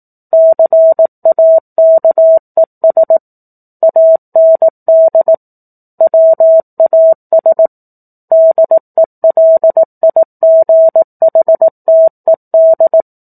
Morse code, heard as a keyed tone, about 18 wpm, 650 Hz. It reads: CAKES AND WAS DELIGHTED
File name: 04-Chapter-04 18 WPM